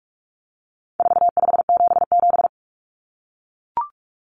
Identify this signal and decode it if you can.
Morse code, heard as a keyed tone, about 44 wpm, 700 Hz. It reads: 4577
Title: Common Cut Numbers - 4-Digit 45wpm